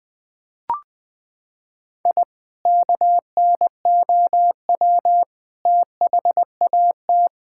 Morse code, heard as a keyed tone, about 20 wpm, 700 Hz. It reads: I KNOW THAT
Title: Sentences with 3-Word Phrases - 20wpm